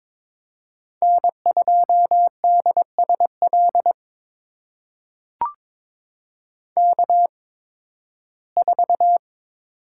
Morse code, heard as a keyed tone, about 22 wpm, spaced as for 14 wpm, 700 Hz. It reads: N2DSL K4
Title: USA Call Signs - 5 Characters - 8x Character Spacing - 22wpm